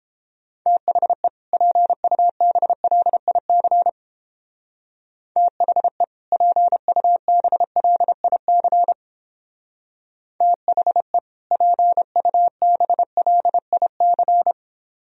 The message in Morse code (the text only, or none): THE PUBLIC THE PUBLIC THE PUBLIC